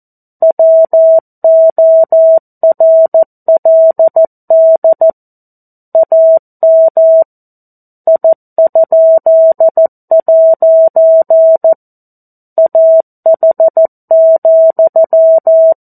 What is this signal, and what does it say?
Morse code, heard as a keyed tone, about 14 wpm, 650 Hz. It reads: WORLD AM I?' AH,
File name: 02-Chapter-02 14 WPM